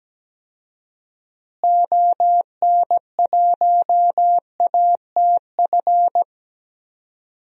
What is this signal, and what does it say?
Morse code, heard as a keyed tone, about 17 wpm, 700 Hz. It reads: ON1ATF
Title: DXCC-100 Call Signs - 6 Characters - Repeated 3x - 17wpm